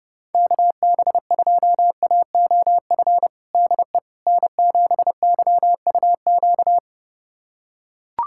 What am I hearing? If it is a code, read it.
Morse code, heard as a keyed tone, about 30 wpm, 700 Hz. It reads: KB2AOF DE N7YUQ
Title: USA Callsign DE Callsign 30wpm